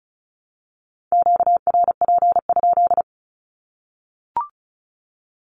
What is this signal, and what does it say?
Morse code, heard as a keyed tone, about 34 wpm, 700 Hz. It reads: QRP?
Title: Q-Codes 35wpm